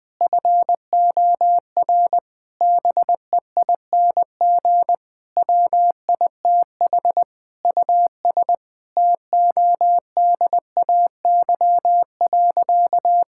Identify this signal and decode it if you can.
Morse code, heard as a keyed tone, about 21 wpm, 700 Hz. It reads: FOR BEING WITH US TODAY.
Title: Sentences with 8-Word Phrases - 20wpm